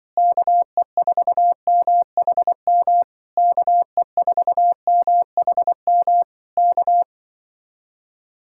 Morse code, heard as a keyed tone, about 25 wpm, 700 Hz. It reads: KE4MHM KE4MHM K
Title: Speed-Racing Common QSO Phrases - 20wpm